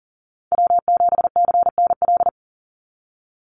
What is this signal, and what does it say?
Morse code, heard as a keyed tone, about 40 wpm, 700 Hz. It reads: W7CNL
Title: NA SOTA Chaser Callsigns 40wpm